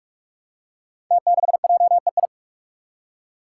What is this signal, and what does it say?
Morse code, heard as a keyed tone, about 44 wpm, 700 Hz. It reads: T6JEI
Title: Five Challenging Letter-Number Combinations 45wpm